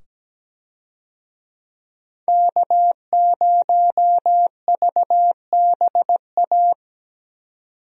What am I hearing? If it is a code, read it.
Morse code, heard as a keyed tone, about 17 wpm, 700 Hz. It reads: K0VBA
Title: USA Call Signs - 5 Characters - 17wpm